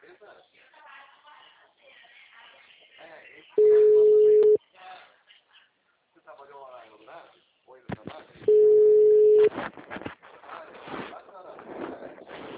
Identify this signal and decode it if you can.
Ringback tone (Europe, 425 Hz), with speech.